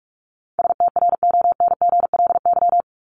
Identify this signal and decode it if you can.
Morse code, heard as a keyed tone, about 44 wpm, 700 Hz. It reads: STRONGLY
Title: LY Suffix - Top 100 Words with Base Words 45wpm